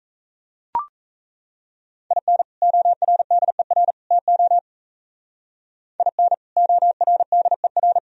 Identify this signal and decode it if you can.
Morse code, heard as a keyed tone, about 35 wpm, 700 Hz. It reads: IN ORDER TO IN ORDER
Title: Speed-Racing - Sets of 3 Words 28wpm